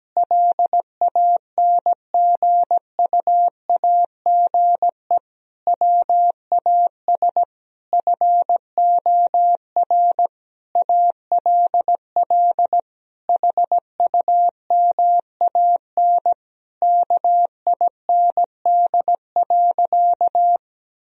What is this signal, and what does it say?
Morse code, heard as a keyed tone, about 17 wpm, 700 Hz. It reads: LANGUAGE WAS FOR ALL HUMAN KIND.
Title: Sentences from Top 1000 Words - Review Part 4 - 17wpm